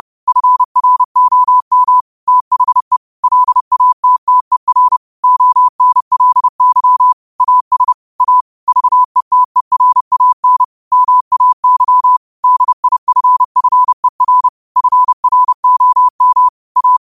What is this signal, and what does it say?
Morse code, heard as a keyed tone, about 30 wpm, 1000 Hz. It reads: FROM THE LATTER ONLY AS A VETERAN MAY DIFFER FROM A
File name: qotdfiles / 30wpm